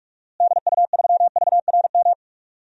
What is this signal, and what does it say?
Morse code, heard as a keyed tone, about 46 wpm, 700 Hz. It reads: DU3VRM
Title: DXCC-100 Call Signs - 6 Characters - Repeated 3x - 45wpm